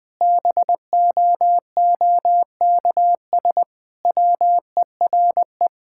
Morse code, heard as a keyed tone, about 20 wpm, 700 Hz. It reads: BOOKS WERE
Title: Sentences with 4-Word Phrases - 20wpm